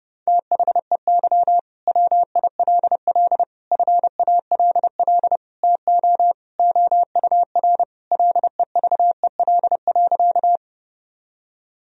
Morse code, heard as a keyed tone, about 30 wpm, 700 Hz. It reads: THEY WILL FALL TO OUR LEVEL.
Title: Sentences from Top 1000 Words - Review Part 4 - 30wpm